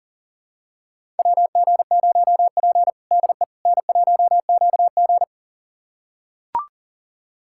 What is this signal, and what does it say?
Morse code, heard as a keyed tone, about 37 wpm, 700 Hz. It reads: WG0P DE N1QG
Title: USA Callsign DE Callsign - 4 Characters - 40wpm